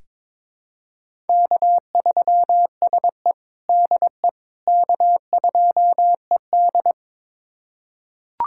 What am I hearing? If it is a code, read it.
Morse code, heard as a keyed tone, about 22 wpm, 700 Hz. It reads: K3SE DE K2ED
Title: USA Callsign DE Callsign - 4 Characters - 22wpm